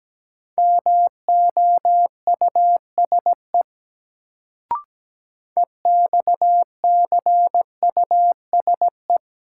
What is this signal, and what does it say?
Morse code, heard as a keyed tone, about 17 wpm, 700 Hz. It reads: MOUSE EXCUSE
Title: Sentences from Top 700 Words - New Word Drill - 17wpm